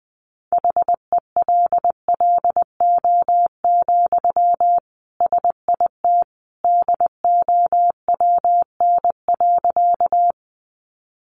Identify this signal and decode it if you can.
Morse code, heard as a keyed tone, about 20 wpm, 700 Hz. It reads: HELLO, SIT DOWN.
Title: Sentences from Top 1100 Words 20wpm